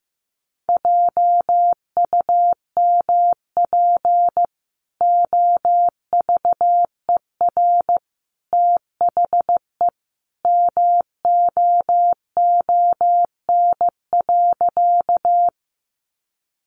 Morse code, heard as a keyed tone, about 15 wpm, 700 Hz. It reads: JUMP OVER THE MOON.
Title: Sentences from Top 1300 Words 15wpm